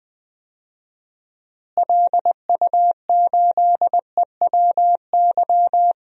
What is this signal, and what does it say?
Morse code, heard as a keyed tone, about 21 wpm, 700 Hz. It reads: LU8EWY
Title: DXCC-100 Call Signs - 6 Characters - Repeated 2x - 20wpm